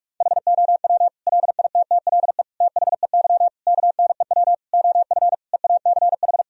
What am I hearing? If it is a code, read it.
Morse code, heard as a keyed tone, about 46 wpm, 700 Hz. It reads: HOW LITTLE THEY KNEW OF EACH
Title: Sentences from Top 1000 Words - Review Part 1 - 45wpm